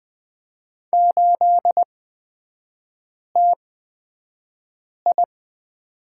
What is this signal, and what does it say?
Morse code, heard as a keyed tone, about 20 wpm, spaced as for 5 wpm, 700 Hz. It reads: 8TI
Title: 224 Lesson 36e - BT - Three Characters 20wpm